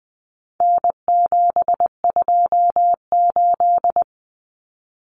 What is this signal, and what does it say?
Morse code, heard as a keyed tone, about 20 wpm, 700 Hz. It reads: N728
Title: Common Cut Numbers - 4-Digit 20wpm